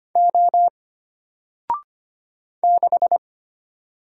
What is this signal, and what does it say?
Morse code, heard as a keyed tone, about 25 wpm, 700 Hz. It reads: O 6
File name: Single-Letter-Number-25wpm